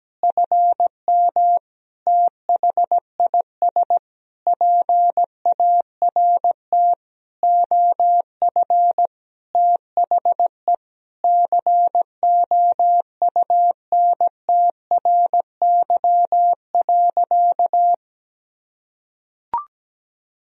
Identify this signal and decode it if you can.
Morse code, heard as a keyed tone, about 17 wpm, 700 Hz. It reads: FM THIS PART OF THE COUNTRY.
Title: Sentences from Top 300 Words and Abbreviations 17wpm